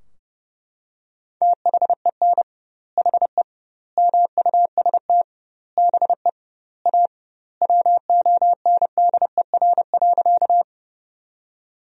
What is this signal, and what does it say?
Morse code, heard as a keyed tone, about 30 wpm, 700 Hz. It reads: THEN HE MUST BE A WONDER.
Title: 2x Word-Spacing - Sentences from Top 500 Words - 30wpm